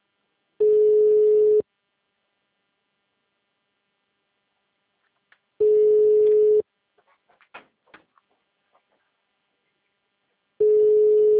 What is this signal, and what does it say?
Ringback tone (Europe, 425 Hz).